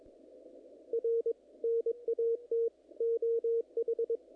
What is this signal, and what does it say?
Morse code, heard as a keyed tone, about 23 wpm, 470 Hz. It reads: R NAT OH
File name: contests / naqp-cw-k5tr-op-k5pi- -1900